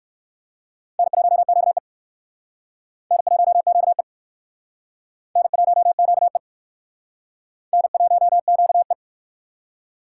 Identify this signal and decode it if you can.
Morse code, heard as a keyed tone, about 56 wpm, 700 Hz. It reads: N1QE N1QE N1QE N1QE
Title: Speed Racing - USA 1x2 Call Signs 45wpm